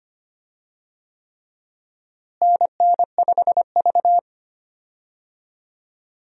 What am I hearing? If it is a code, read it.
Morse code, heard as a keyed tone, about 25 wpm, 700 Hz. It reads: NN5V
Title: USA Call Signs - 4 Characters - Repeated 3x - 25wpm